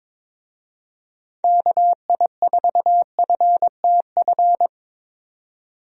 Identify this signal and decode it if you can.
Morse code, heard as a keyed tone, about 22 wpm, 700 Hz. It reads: KI4FTF
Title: USA Call Signs - 6 Characters - 22wpm